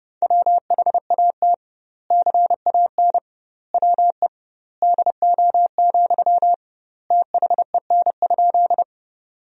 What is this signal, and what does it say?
Morse code, heard as a keyed tone, about 32 wpm, 700 Hz. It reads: WHAT CAN WE DO, THEN?
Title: 2x Word-Spacing - Sentences from Top 100 Words - 30wpm